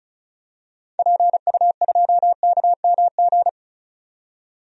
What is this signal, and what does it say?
Morse code, heard as a keyed tone, about 34 wpm, 700 Hz. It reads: PU2KMG
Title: DXCC-100 Call Signs - 6 Characters - 35wpm